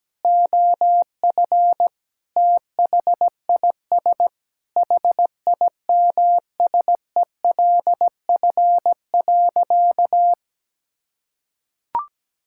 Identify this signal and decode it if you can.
Morse code, heard as a keyed tone, about 17 wpm, 700 Hz. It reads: OF THIS HIMSELF.
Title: Sentences from Top 1600 Words 17wpm